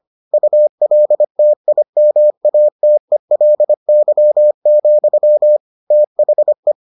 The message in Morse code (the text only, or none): ULTIMATELY, THE